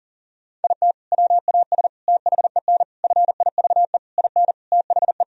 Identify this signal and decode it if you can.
Morse code, heard as a keyed tone, about 37 wpm, 700 Hz. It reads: IT WAS THEN FIVE IN THE
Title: Sentences from Top 1000 Words - Review Part 2 - 40wpm